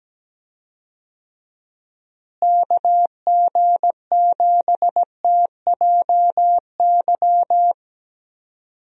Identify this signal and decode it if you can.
Morse code, heard as a keyed tone, about 17 wpm, 700 Hz. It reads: KG7TJY
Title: USA Call Signs - 6 Characters - Repeated 3x - 17wpm